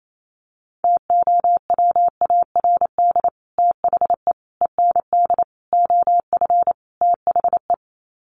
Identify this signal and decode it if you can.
Morse code, heard as a keyed tone, about 28 wpm, 700 Hz. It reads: TOWARD THE END OF THE
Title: Sets of 5 Words 28wpm